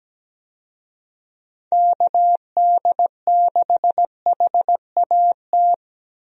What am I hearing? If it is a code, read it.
Morse code, heard as a keyed tone, about 17 wpm, 700 Hz. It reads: KD6HAT
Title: USA Call Signs - 6 Characters - 17wpm